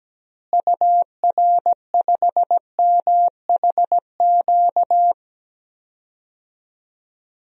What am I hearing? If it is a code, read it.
Morse code, heard as a keyed tone, about 17 wpm, 700 Hz. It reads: UR5MHQ
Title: DXCC-100 Call Signs - 6 Characters - Repeated 3x - 17wpm